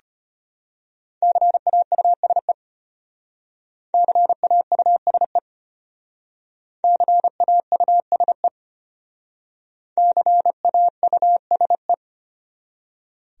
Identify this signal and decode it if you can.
Morse code, heard as a keyed tone, about 32 wpm, 700 Hz. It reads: CAUSE CAUSE CAUSE CAUSE